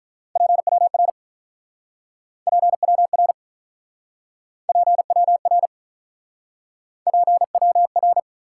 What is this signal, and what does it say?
Morse code, heard as a keyed tone, about 48 wpm, 700 Hz. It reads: PWR PWR PWR PWR